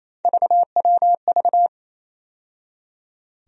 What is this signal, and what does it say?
Morse code, heard as a keyed tone, about 28 wpm, 700 Hz. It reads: VWV
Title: USA Call Sign Postfixes - 3 Letters 28wpm